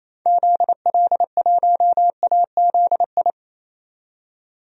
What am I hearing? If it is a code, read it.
Morse code, heard as a keyed tone, about 28 wpm, 700 Hz. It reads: ZL1AZI
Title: DXCC-100 Call Signs - 6 Characters - 28wpm